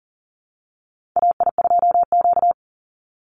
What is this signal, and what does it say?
Morse code, heard as a keyed tone, about 40 wpm, 700 Hz. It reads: AI2Q
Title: Call Signs Repeated 3x - 40wpm